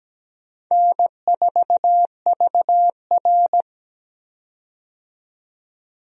Morse code, heard as a keyed tone, about 17 wpm, 700 Hz. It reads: N4VR
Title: USA Call Signs - 4 Characters - 17wpm